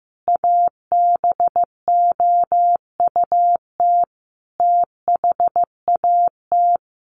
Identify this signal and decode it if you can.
Morse code, heard as a keyed tone, about 15 wpm, 700 Hz. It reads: ABOUT THAT
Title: Sets of 5 Words Encore2 15wpm